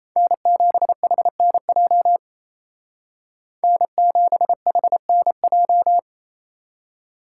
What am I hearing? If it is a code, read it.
Morse code, heard as a keyed tone, about 36 wpm, 700 Hz. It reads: N7HNJ N7HNJ